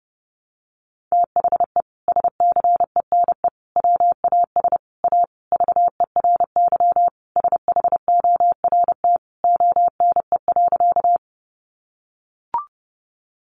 Morse code, heard as a keyed tone, about 30 wpm, 700 Hz. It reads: THE SCENE WAS A VERY SHORT ONE.
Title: Sentences from Top 1100 Words 30wpm